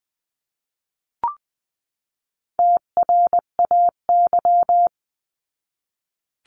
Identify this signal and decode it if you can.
Morse code, heard as a keyed tone, about 20 wpm, 700 Hz. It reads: TRAY
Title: 259 Lesson 42d - Words 18-20wpm